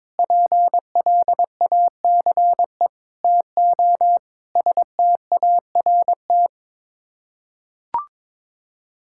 Morse code, heard as a keyed tone, about 22 wpm, 700 Hz. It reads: PLACE TO START
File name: practice / 22wpm / Sets of 6 Words - 22wpm